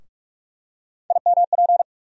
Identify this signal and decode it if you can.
Morse code, heard as a keyed tone, about 42 wpm, 700 Hz. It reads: IMP
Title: USA Call Sign Postfixes - 3 Letters 45wpm